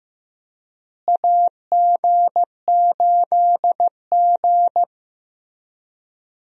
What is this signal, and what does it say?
Morse code, heard as a keyed tone, about 16 wpm, 700 Hz. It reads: AG8G
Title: USA Call Signs - 4 Characters - Repeated 3x - 15wpm